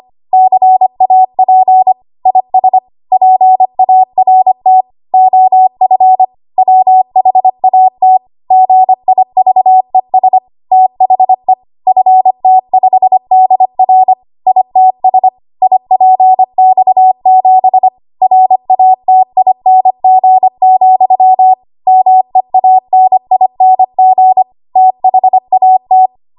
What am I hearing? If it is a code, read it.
Morse code, heard as a keyed tone, about 25 wpm, 750 Hz. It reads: CAP IS PART OF WHAT GIVES THE FT5DR ITS IPX7 RATING, MEANING THAT
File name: Archive / 25 WPM / 25WPM